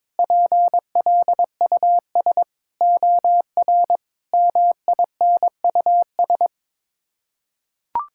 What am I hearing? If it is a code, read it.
Morse code, heard as a keyed tone, about 22 wpm, 700 Hz. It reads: PLUS OR MINUS
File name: practice / 22wpm / Sets of 8 Words - 22wpm